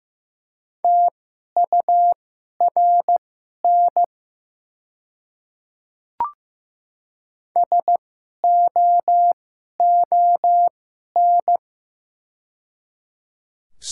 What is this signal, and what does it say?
Morse code, heard as a keyed tone, about 15 wpm, spaced as for 11 wpm, 690 Hz. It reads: TURN SOON
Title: 2x Character Spacing - Top 500 Words - 15wpm